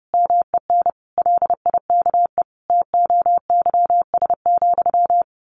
Morse code, heard as a keyed tone, about 30 wpm, 700 Hz. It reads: MEN LIKE TOYS,
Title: A Sound Of Thunder 30wpm